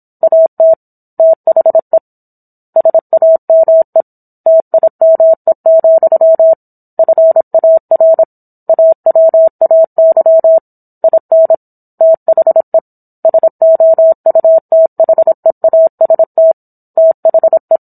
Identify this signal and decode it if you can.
Morse code, heard as a keyed tone, about 27 wpm, 650 Hz. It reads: AT THE SAME TIME, FAR AWAY IN THE SOUTHEAST THE